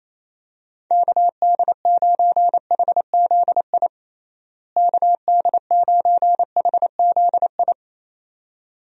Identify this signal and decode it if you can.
Morse code, heard as a keyed tone, about 28 wpm, 700 Hz. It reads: KD9HZI KD9HZI
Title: USA Call Signs - 6 Characters - Repeated 2x - 28wpm